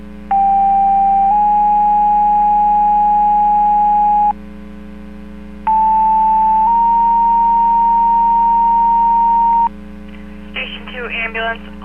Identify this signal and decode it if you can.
Two-tone page, heard as a one-second tone, then a three-second tone, about 790 then 830 Hz, about 880 then 930 Hz, with speech. It is heard from 0.3 to 9.7 s.